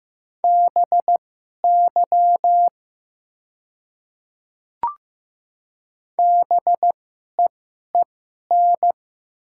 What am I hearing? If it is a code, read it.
Morse code, heard as a keyed tone, about 15 wpm, spaced as for 11 wpm, 700 Hz. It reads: BY BEEN
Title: 2x Character Spacing - Top 500 Words - 15wpm